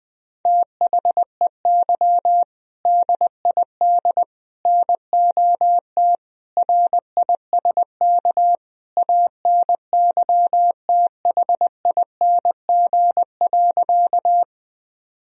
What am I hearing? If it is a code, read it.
Morse code, heard as a keyed tone, about 20 wpm, 700 Hz. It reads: THEY DID NOT RISK ANYTHING.
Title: Sentences from Top 1000 Words - Review Part 4 - 20wpm